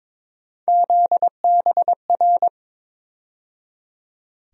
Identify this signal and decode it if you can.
Morse code, heard as a keyed tone, about 22 wpm, 700 Hz. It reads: ZBR
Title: USA Call Sign Postfixes - 3 Letters 22wpm